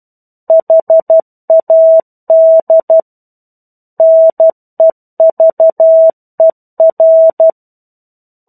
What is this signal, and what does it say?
Morse code, heard as a keyed tone, about 12 wpm, 650 Hz. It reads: HAD NEVER
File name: 01-Chapter-01 12 WPM